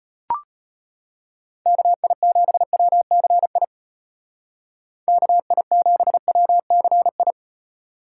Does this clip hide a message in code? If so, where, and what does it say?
Yes: Morse code from 1.7 to 7.3 s, heard as a keyed tone, about 38 wpm, 700 Hz. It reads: KI7WCI KI7WCI